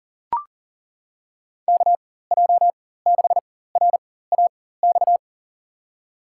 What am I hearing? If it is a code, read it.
Morse code, heard as a keyed tone, about 40 wpm, spaced as for 19 wpm, 700 Hz. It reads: KJ6RAX
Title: USA Call Signs - 6 Characters - 4x Character Spacing - 40wpm